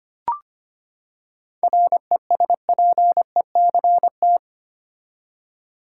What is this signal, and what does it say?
Morse code, heard as a keyed tone, about 25 wpm, 700 Hz. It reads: RESPECT
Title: Seven Letter Words - 25wpm